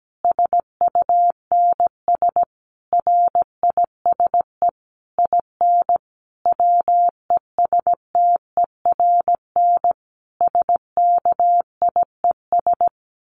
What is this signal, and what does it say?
Morse code, heard as a keyed tone, about 17 wpm, 700 Hz. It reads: SUNS RISE IN WESTERN SKIES
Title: A Sound Of Thunder 17wpm